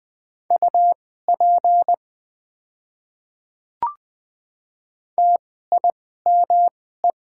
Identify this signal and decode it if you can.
Morse code, heard as a keyed tone, about 20 wpm, spaced as for 14 wpm, 700 Hz. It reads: UP TIME
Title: 2x Character Spacing - Top 200 Words - 20wpm